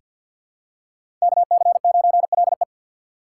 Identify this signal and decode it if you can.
Morse code, heard as a keyed tone, about 47 wpm, 700 Hz. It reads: KC9LE
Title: USA Call Signs - 5 Characters - 2x Character Spacing - 50wpm